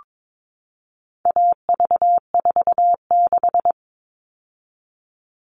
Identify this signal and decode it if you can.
Morse code, heard as a keyed tone, about 22 wpm, 700 Hz. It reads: AV46
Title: All Cut Numbers - 4-Digit 22wpm